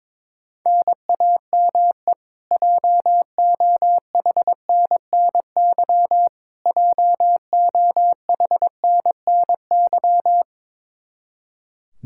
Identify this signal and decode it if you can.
Morse code, heard as a keyed tone, about 22 wpm, 700 Hz. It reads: NAME JOHNNY JOHNNY
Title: Practice QSO - Exchange 1 - Beginning - 22wpm